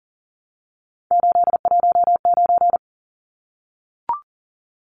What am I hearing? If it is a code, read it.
Morse code, heard as a keyed tone, about 40 wpm, 700 Hz. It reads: 819
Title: Numbers - 3-Digit 40wpm